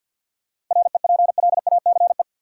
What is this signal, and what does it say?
Morse code, heard as a keyed tone, about 50 wpm, 700 Hz. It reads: REPLACE
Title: Seven Letter Words - 50wpm